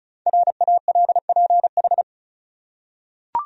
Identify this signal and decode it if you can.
Morse code, heard as a keyed tone, about 36 wpm, 700 Hz. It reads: RALPH
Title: USA Common Names - Max 5 Letters - 35wpm